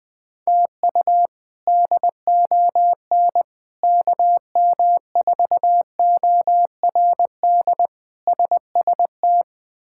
Morse code, heard as a keyed tone, about 20 wpm, 700 Hz. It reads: TU DON KM4ORD SST
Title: K1USN SST QSO Practice - 20wpm